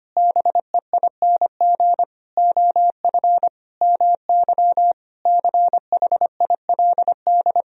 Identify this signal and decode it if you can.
Morse code, heard as a keyed tone, about 25 wpm, 700 Hz. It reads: BEING OF MY CHILD
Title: Sentences from Top 1000 Words - Review Part 3 - 25wpm